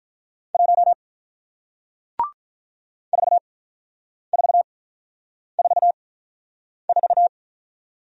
Morse code, heard as a keyed tone, about 48 wpm, spaced as for 9 wpm, 700 Hz. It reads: 1 4444